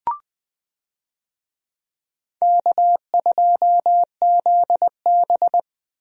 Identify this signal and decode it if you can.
Morse code, heard as a keyed tone, about 21 wpm, 700 Hz. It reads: K2ZB
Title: USA Call Signs - 4 Characters - Repeated 3x - 20wpm